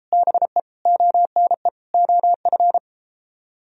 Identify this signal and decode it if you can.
Morse code, heard as a keyed tone, about 31 wpm, 700 Hz. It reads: BE ONE OF